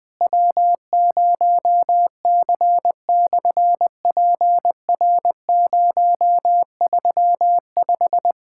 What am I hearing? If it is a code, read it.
Morse code, heard as a keyed tone, about 20 wpm, 700 Hz. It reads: W0C/PR035
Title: SOTA QSO Practice 20wpm